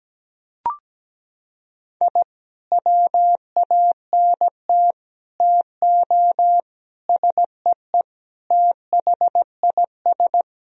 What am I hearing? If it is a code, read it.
Morse code, heard as a keyed tone, about 17 wpm, 700 Hz. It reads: I WANT TO SEE THIS
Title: Sentences from Top 1000 Words - Review Part 3 - 17wpm